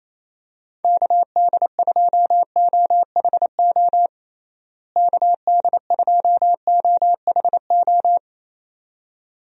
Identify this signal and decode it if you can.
Morse code, heard as a keyed tone, about 28 wpm, 700 Hz. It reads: KD2OHO KD2OHO
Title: USA Call Signs - 6 Characters - Repeated 2x - 28wpm